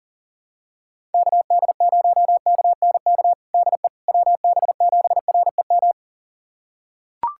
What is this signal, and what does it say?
Morse code, heard as a keyed tone, about 40 wpm, 700 Hz. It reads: KD0KNK DE WB7REM
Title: USA Callsign DE Callsign - 6 Characters - 40wpm